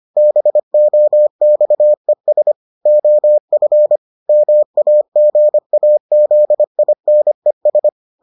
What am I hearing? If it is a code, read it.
Morse code, heard as a keyed tone, about 25 wpm, 600 Hz. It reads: BOXES OF MAGAZINES